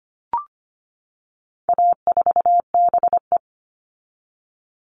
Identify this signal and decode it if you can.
Morse code, heard as a keyed tone, about 25 wpm, 700 Hz. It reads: A4BE
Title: All Cut Numbers - 4-Digit 25wpm